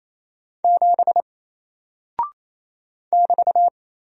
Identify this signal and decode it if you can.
Morse code, heard as a keyed tone, about 27 wpm, 700 Hz. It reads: 7 <BT>
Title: Single Character (Letter:Number:Punctuation:Prosign) 28wpm